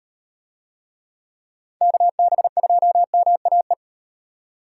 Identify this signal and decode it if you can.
Morse code, heard as a keyed tone, about 38 wpm, 700 Hz. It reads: KB2MAE